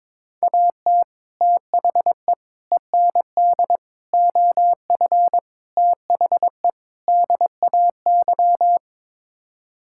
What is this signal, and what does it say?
Morse code, heard as a keyed tone, about 22 wpm, 700 Hz. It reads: AT THE END OF THE DAY
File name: practice / 22wpm / Sets of 6 Words - 22wpm